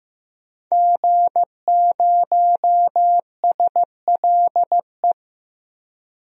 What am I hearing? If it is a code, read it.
Morse code, heard as a keyed tone, about 15 wpm, 700 Hz. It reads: G0SLE
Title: DXCC-100 Call Signs - 5 Characters - Repeated 3x - 15wpm